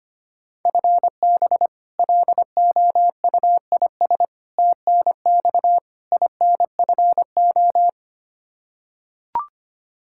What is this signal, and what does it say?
Morse code, heard as a keyed tone, about 25 wpm, 700 Hz. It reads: FB LOUIS TNX INFO
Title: K1USN SST QSO Practice 25wpm